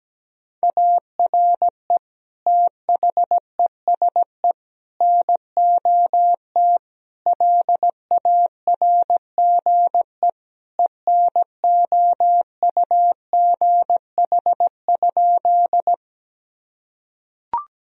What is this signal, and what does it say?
Morse code, heard as a keyed tone, about 17 wpm, 700 Hz. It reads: ARE THESE NOT LARGE ENOUGH?
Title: Sentences from Top 1000 Words - Review Part 1 - 17wpm